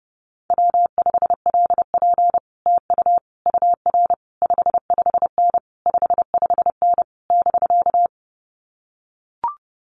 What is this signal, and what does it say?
Morse code, heard as a keyed tone, about 30 wpm, 700 Hz. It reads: W5LP TU UR 55N 55N <BK>
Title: Common QSO Phrases Encore 30wpm